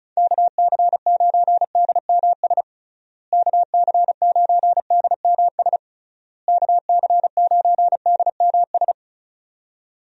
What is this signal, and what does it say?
Morse code, heard as a keyed tone, about 34 wpm, 700 Hz. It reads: KC9DMS KC9DMS KC9DMS
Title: USA Call Signs - 6 Characters - Repeated 3x - 35wpm